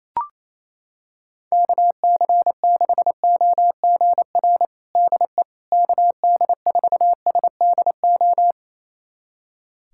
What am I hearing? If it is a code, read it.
Morse code, heard as a keyed tone, about 28 wpm, 700 Hz. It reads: KC6OGR DE KD4SDO
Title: USA Callsign DE Callsign 28wpm